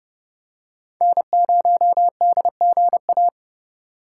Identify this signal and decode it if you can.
Morse code, heard as a keyed tone, about 30 wpm, 700 Hz. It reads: N0DGA